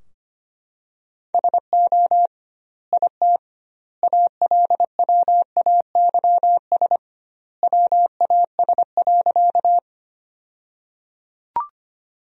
Morse code, heard as a keyed tone, about 25 wpm, 700 Hz. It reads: SO IT ALWAYS WAS.
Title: 2x Word-Spacing - Sentences from Top 300 Words - 25wpm